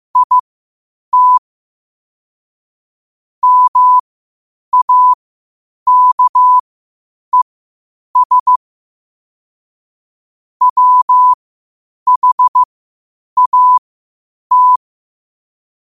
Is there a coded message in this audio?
Yes: Morse code from 0.2 to 14.8 s, heard as a keyed tone, about 15 wpm, spaced as for 8 wpm, 1000 Hz. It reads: IT MAKES WHAT